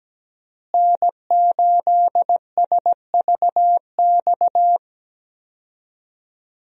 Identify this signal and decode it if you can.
Morse code, heard as a keyed tone, about 17 wpm, 700 Hz. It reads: N8SVX
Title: USA Call Signs - 5 Characters - Repeated 2x - 17wpm